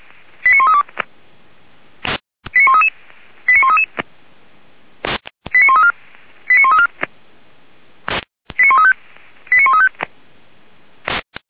Five-tone selective call (ZVEI), heard as quick runs of five tones, tones of about 80 ms. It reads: 89132 89133 89133 89134 89134 89135 89135
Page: ZVEI